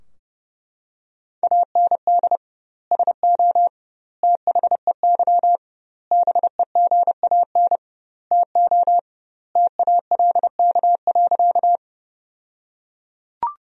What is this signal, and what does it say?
Morse code, heard as a keyed tone, about 30 wpm, 700 Hz. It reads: AND SO THEY BEGAN TO TALK.
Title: 2x Word-Spacing - Sentences from Top 400 Words - 30wpm